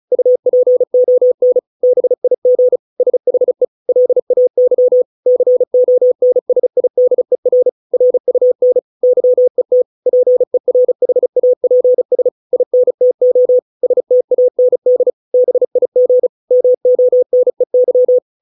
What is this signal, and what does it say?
Morse code, heard as a keyed tone, about 34 wpm, 500 Hz. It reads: UPON BIG SHE LAY CONSIDER RUN YET PERHAPS INTO STAND BIG MONEY